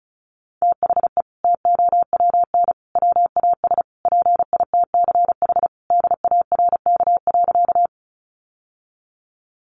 Morse code, heard as a keyed tone, about 34 wpm, 700 Hz. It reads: THE TOWN WAS PITCH DARK.
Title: Sentences from Top 1300 Words 35wpm